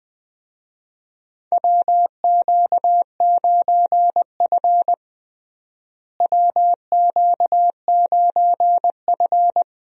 Morse code, heard as a keyed tone, about 20 wpm, 700 Hz. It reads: WQ9F WQ9F
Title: USA Call Signs - 4 Characters - Repeated 2x - 20wpm